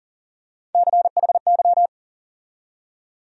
Morse code, heard as a keyed tone, about 40 wpm, 700 Hz. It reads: CHY
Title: USA Call Sign Postfixes - 3 Letters 40wpm